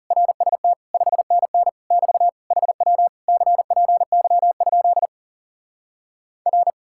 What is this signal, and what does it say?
Morse code, heard as a keyed tone, about 40 wpm, 700 Hz. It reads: RST 5NN <BT> HW CPY? R